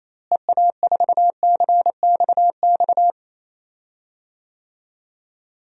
Morse code, heard as a keyed tone, about 28 wpm, 700 Hz. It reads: EA4CXX
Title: DXCC-100 Call Signs - 6 Characters - Repeated 3x - 28wpm